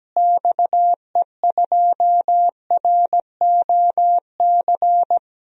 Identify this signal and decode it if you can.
Morse code, heard as a keyed tone, about 17 wpm, 700 Hz. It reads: XE2ROC
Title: DXCC-100 Call Signs - 6 Characters - Repeated 2x - 17wpm